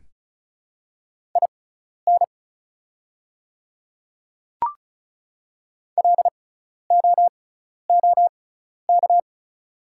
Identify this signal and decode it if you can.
Morse code, heard as a keyed tone, about 36 wpm, spaced as for 12 wpm, 700 Hz. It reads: IN LOOK
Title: 6x Character Spacing - Top 100 Words - 35wpm